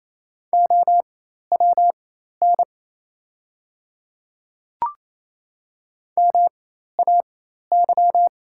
Morse code, heard as a keyed tone, about 27 wpm, spaced as for 13 wpm, 700 Hz. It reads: OWN MAY
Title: 4x Character Spacing - Top 300 Words - 28wpm